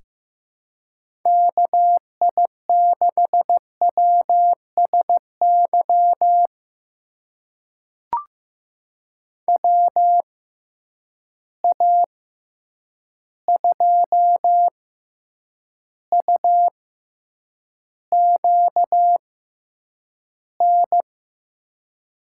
Morse code, heard as a keyed tone, about 15 wpm, spaced as for 8 wpm, 700 Hz. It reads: KI6WSY WA2UQN
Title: USA Call Signs - 6 Characters - 6x Character Spacing - 15wpm